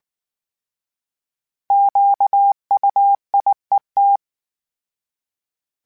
Morse code, heard as a keyed tone, about 19 wpm, 800 Hz. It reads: QUIET